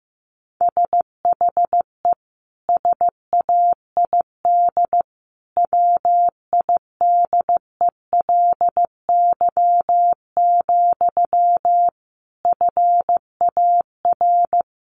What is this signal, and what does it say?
Morse code, heard as a keyed tone, about 15 wpm, 700 Hz. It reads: SHE SAID WIDELY, FAR
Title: Sentences from Top 1200 Words 15wpm